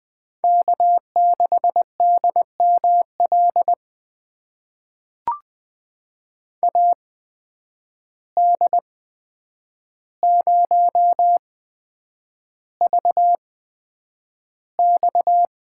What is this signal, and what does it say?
Morse code, heard as a keyed tone, about 20 wpm, spaced as for 9 wpm, 700 Hz. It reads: K6DML AD0VX
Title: USA Call Signs - 5 Characters - 8x Character Spacing - 20wpm